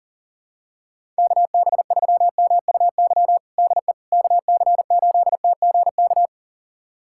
Morse code, heard as a keyed tone, about 40 wpm, 690 Hz. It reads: KB3MUY DE KC8TGK
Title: USA Callsign DE Callsign - 6 Characters - 40wpm